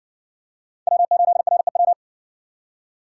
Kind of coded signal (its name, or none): Morse code